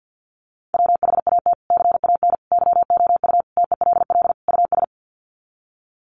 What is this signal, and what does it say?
Morse code, heard as a keyed tone, about 50 wpm, 700 Hz. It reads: WHAT CAN YOU TELL US
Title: Sets of 5 Words Encore 50wpm